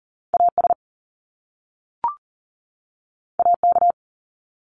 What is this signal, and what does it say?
Morse code, heard as a keyed tone, about 41 wpm, 700 Hz. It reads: AS AK
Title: Top Two Letter Combinations in Words 40wpm